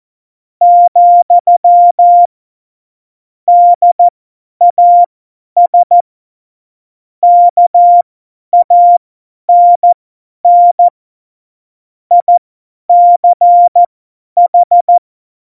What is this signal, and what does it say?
Morse code, heard as a keyed tone, about 14 wpm, spaced as for 10 wpm, 700 Hz. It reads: , DAS KANN ICH
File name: Morsekurs / Witze / Witz 08. Dezember 2020, 50 F70